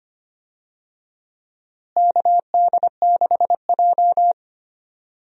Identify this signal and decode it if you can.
Morse code, heard as a keyed tone, about 25 wpm, 700 Hz. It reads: KD6J
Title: USA Call Signs - 4 Characters - Repeated 3x - 25wpm